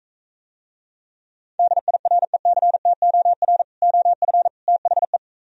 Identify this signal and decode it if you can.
Morse code, heard as a keyed tone, about 44 wpm, 700 Hz. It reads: DIRECTOR OF THE